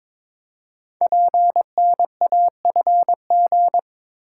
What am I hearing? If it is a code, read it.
Morse code, heard as a keyed tone, about 22 wpm, 700 Hz. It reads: PNAFG
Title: Code Groups - 5 Letters - 22wpm